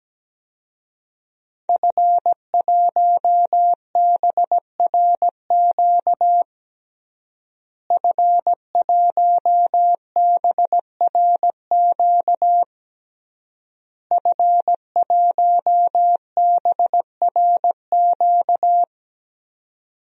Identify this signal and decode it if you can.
Morse code, heard as a keyed tone, about 17 wpm, 700 Hz. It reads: F1BRQ F1BRQ F1BRQ
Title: DXCC-100 Call Signs - 5 Characters - Repeated 3x - 17wpm